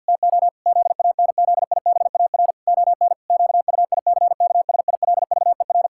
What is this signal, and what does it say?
Morse code, heard as a keyed tone, about 50 wpm, 700 Hz. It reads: TO ZANZIBAR ON QUICKSILVER
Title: Pangram Challenge v2 - 50wpm